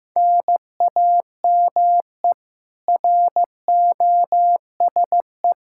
Morse code, heard as a keyed tone, about 15 wpm, 700 Hz. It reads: NAME ROSE
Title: Practice QSO - Exchange 1 - Beginning - 15wpm